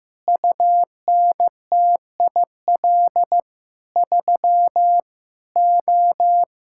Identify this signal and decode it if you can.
Morse code, heard as a keyed tone, about 15 wpm, 700 Hz. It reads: UNTIL 3 O
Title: Pangram Challenge v2 - 15wpm